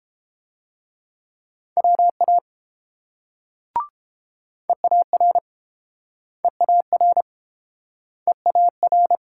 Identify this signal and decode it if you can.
Morse code, heard as a keyed tone, about 30 wpm, 700 Hz. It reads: WA EAR EAR EAR